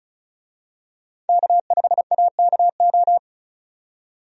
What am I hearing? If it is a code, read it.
Morse code, heard as a keyed tone, about 36 wpm, 700 Hz. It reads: K5AKO
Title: USA Call Signs - 5 Characters - 35wpm